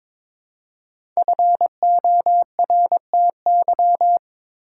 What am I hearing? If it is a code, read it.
Morse code, heard as a keyed tone, about 22 wpm, 700 Hz. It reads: FORTY
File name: 5 Letter Words - First 3 Letters Given - 22wpm